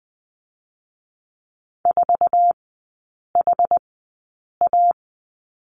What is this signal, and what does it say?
Morse code, heard as a keyed tone, about 20 wpm, spaced as for 8 wpm, 700 Hz. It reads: 4HA
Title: 074 Lesson 13e - Letter L - Three Characters 20wpm